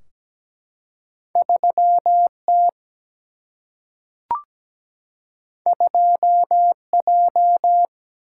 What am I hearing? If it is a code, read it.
Morse code, heard as a keyed tone, about 17 wpm, 700 Hz. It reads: 3T 2J
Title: Single-Letter Single-Number 17wpm